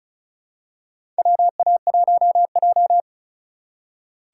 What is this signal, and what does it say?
Morse code, heard as a keyed tone, about 34 wpm, 700 Hz. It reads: WA1J
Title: USA Call Signs - 4 Characters - 4x Character Spacing - 35wpm